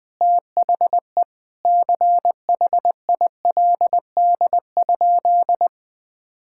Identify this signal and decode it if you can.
Morse code, heard as a keyed tone, about 21 wpm, 700 Hz. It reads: THE CHILD?
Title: Sentences from Top 1600 Words 20wpm